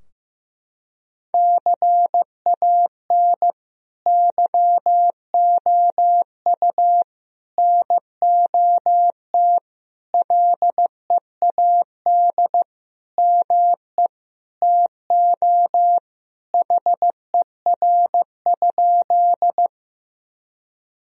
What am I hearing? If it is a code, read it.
Morse code, heard as a keyed tone, about 15 wpm, 700 Hz. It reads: CAN YOU NOT LEAD ME TO HER?
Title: Sentences from Top 1000 Words - Review Part 2 - 15wpm